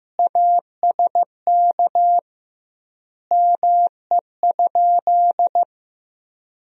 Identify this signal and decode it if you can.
Morse code, heard as a keyed tone, about 15 wpm, 700 Hz. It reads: ASK ME?
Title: 2x Word-Spacing - Sentences from Top 200 Words - 15wpm